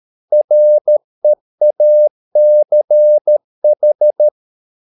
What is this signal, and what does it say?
Morse code, heard as a keyed tone, about 13 wpm, 600 Hz. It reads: REACH